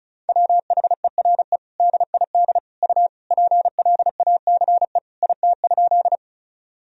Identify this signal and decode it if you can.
Morse code, heard as a keyed tone, about 36 wpm, 700 Hz. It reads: WHERE DID U PLACE IT?
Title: Sentences from Top 200 Words and Abbreviations 35wpm